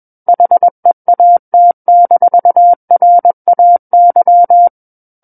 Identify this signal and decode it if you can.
Morse code, heard as a keyed tone, about 21 wpm, 700 Hz. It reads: HEAT-RAY
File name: Book One / 12-Chapter 12 - 21 WPM